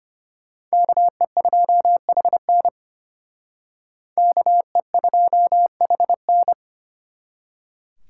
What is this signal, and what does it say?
Morse code, heard as a keyed tone, about 32 wpm, 700 Hz. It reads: KE2HN KE2HN